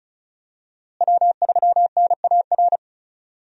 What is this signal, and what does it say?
Morse code, heard as a keyed tone, about 36 wpm, 700 Hz. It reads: W3NAR
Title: USA Call Signs - 5 Characters - Repeated 3x - 35wpm